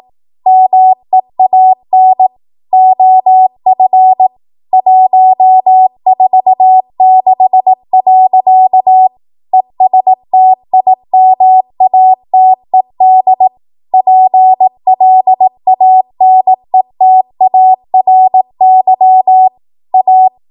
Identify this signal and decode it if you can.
Morse code, heard as a keyed tone, about 18 wpm, 750 Hz. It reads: MEAN OF 146. ESTIMATED PLANETARY A
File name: Archive / 18 WPM / 18WPM